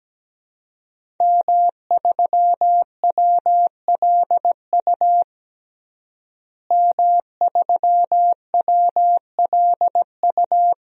Morse code, heard as a keyed tone, about 17 wpm, 700 Hz. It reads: M3WLU M3WLU
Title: DXCC-100 Call Signs - 5 Characters - Repeated 3x - 17wpm